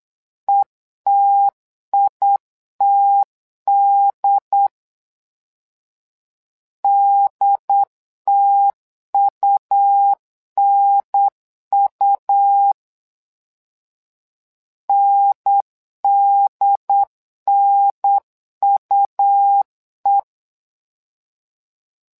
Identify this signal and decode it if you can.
Morse code, heard as a keyed tone, about 9 wpm, 800 Hz. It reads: ETITD DTUNU NDNUE
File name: LEZIONI / 2 UHDN / 2 UHDN 8wpm2232